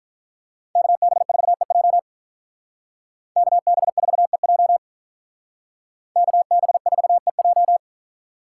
Morse code, heard as a keyed tone, about 45 wpm, 700 Hz. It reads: KB4EJ KB4EJ KB4EJ